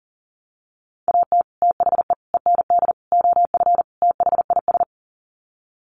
Morse code, heard as a keyed tone, about 40 wpm, 700 Hz. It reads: AT THE END OF THIS
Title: Sets of 5 Words 40wpm